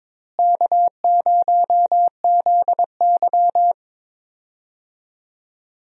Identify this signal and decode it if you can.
Morse code, heard as a keyed tone, about 22 wpm, 700 Hz. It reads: K0ZY
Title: USA Call Signs - 4 Characters - Repeated 2x - 22wpm